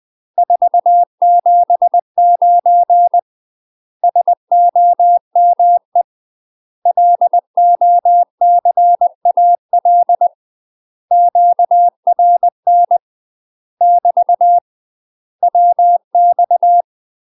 Morse code, heard as a keyed tone, about 20 wpm, 700 Hz. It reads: 479 SOME LOCAL QRN <BT> WX
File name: UK MASTER QSOs TRAINING / 20WPM / UK MASTER QSOs 0063